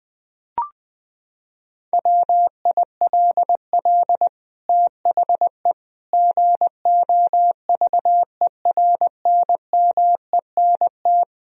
Morse code, heard as a keyed tone, about 21 wpm, 700 Hz. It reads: WILL THE GOVERNMENT
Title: Sentences from Top 1000 Words - Review Part 3 - 20wpm